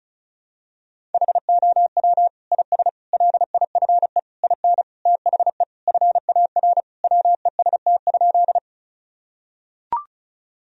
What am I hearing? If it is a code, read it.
Morse code, heard as a keyed tone, about 34 wpm, 700 Hz. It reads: HOW IS LIFE IN THE FAR WEST?
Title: Sentences from Top 1000 Words - Review Part 2 - 35wpm